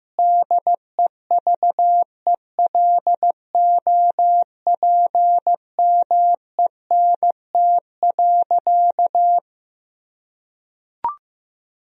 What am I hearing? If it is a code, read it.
Morse code, heard as a keyed tone, about 14 wpm, 700 Hz. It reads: DEVELOPMENT.
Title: Sentences from Top 1000 Words - Review Part 4 - 15wpm